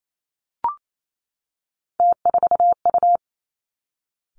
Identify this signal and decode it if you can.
Morse code, heard as a keyed tone, about 28 wpm, 700 Hz. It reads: T4U
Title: Three Challenging Letter-Number Combinations 28wpm